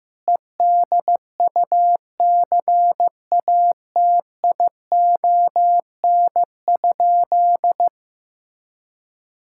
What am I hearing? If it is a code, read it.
Morse code, heard as a keyed tone, about 15 wpm, 700 Hz. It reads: EDUCATION?
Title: Sentences from Top 1000 Words - Review Part 6 - 15wpm